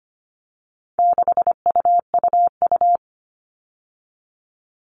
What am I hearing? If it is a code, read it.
Morse code, heard as a keyed tone, about 25 wpm, 700 Hz. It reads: 6UUU
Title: All Cut Numbers - 4-Digit 25wpm